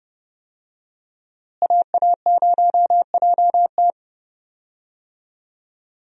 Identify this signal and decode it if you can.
Morse code, heard as a keyed tone, about 30 wpm, 700 Hz. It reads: AA0JT
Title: USA Call Signs - 5 Characters - Repeated 2x - 30wpm